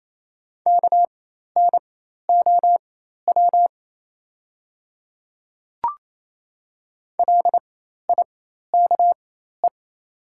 Morse code, heard as a keyed tone, about 28 wpm, spaced as for 13 wpm, 700 Hz. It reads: KNOW LIKE
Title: 4x Character Spacing - Top 100 Words - 28wpm